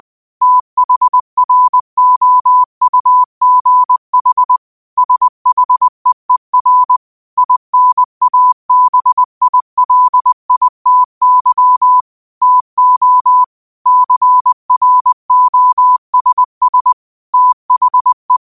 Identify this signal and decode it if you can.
Morse code, heard as a keyed tone, about 20 wpm, 1000 Hz. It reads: THROUGH SHEER INABILITY TO CROSS THE
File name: qotdfiles / 20wpm